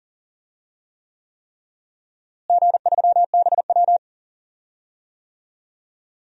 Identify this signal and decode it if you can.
Morse code, heard as a keyed tone, about 40 wpm, 700 Hz. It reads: G3BW
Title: DXCC-100 Call Signs - 4 Characters - Repeated 3x - 40wpm